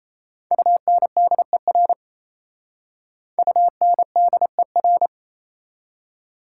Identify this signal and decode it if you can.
Morse code, heard as a keyed tone, about 37 wpm, 700 Hz. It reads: UNDER UNDER